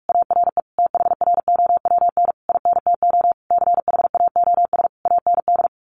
Morse code, heard as a keyed tone, about 44 wpm, 700 Hz. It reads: ARE THROWN INTO CHAOS AND
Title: A Sound Of Thunder 45wpm